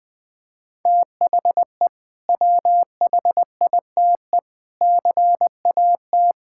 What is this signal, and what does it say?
Morse code, heard as a keyed tone, about 20 wpm, 700 Hz. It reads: THE WHITE CAT
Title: Sentences with 2-Word Phrases - 20wpm